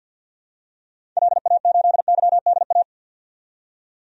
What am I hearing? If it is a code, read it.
Morse code, heard as a keyed tone, about 53 wpm, 700 Hz. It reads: LA8YDA
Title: DXCC-100 Call Signs - 6 Characters - 50wpm